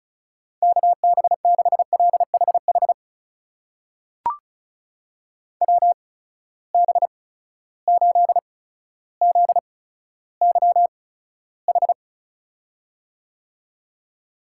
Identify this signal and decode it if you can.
Morse code, heard as a keyed tone, about 34 wpm, spaced as for 15 wpm, 700 Hz. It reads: KB6LHH WB8ZYH
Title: USA Call Signs - 6 Characters - 8x Character Spacing - 35wpm